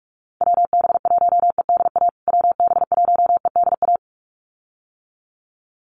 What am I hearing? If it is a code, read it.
Morse code, heard as a keyed tone, about 44 wpm, 700 Hz. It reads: WB1EDA WB1EDA
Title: USA Call Signs Repeated 2x 45wpm